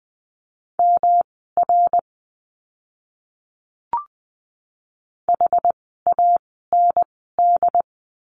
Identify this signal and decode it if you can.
Morse code, heard as a keyed tone, about 20 wpm, spaced as for 14 wpm, 700 Hz. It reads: MR HAND
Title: 244 Lesson 39f - Letter z - Words 20wpm